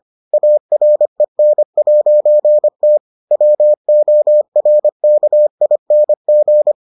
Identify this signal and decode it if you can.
Morse code, heard as a keyed tone, about 25 wpm, 600 Hz. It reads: AREN'T WORKING